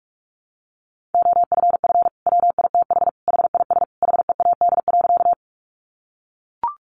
Morse code, heard as a keyed tone, about 44 wpm, 700 Hz. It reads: OFF WITH HIS HEAD.
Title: Sentences from Top 300 Words 45wpm